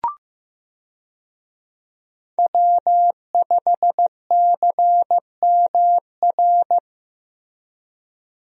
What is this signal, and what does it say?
Morse code, heard as a keyed tone, about 15 wpm, 700 Hz. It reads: W5CMR
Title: USA Call Signs - 5 Characters - Repeated 3x - 15wpm